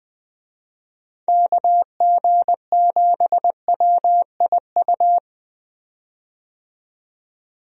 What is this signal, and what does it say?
Morse code, heard as a keyed tone, about 20 wpm, 700 Hz. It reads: KG7WIU
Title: USA Call Signs - 6 Characters - 20wpm